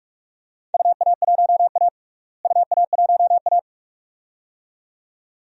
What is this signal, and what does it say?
Morse code, heard as a keyed tone, about 42 wpm, 700 Hz. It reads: UA1A UA1A
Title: DXCC-100 Call Signs - 4 Characters - Repeated 2x - 45wpm